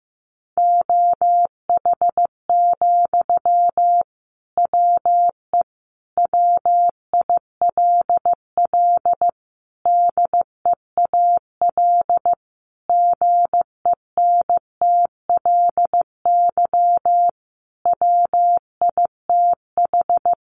Morse code, heard as a keyed tone, about 15 wpm, 700 Hz. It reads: OH, WE WILL DEAL GENTLY WITH
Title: Sentences from Top 1500 Words 15wpm